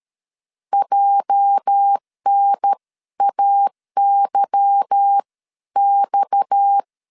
Morse code, heard as a keyed tone, about 13 wpm, spaced as for 11 wpm, 800 Hz. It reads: JNAYX
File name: LEZIONI / 9 2867 12wpm 2322